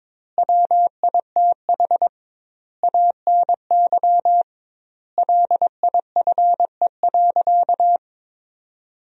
Morse code, heard as a keyed tone, about 22 wpm, 700 Hz. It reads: WITH ANY LIFE.
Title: 2x Word-Spacing - Sentences from Top 300 Words - 22wpm